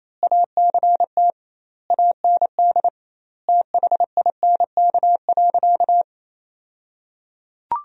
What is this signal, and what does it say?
Morse code, heard as a keyed tone, about 28 wpm, 700 Hz. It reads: ACT AND THINK.
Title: 2x Word-Spacing - Sentences from Top 500 Words - 28wpm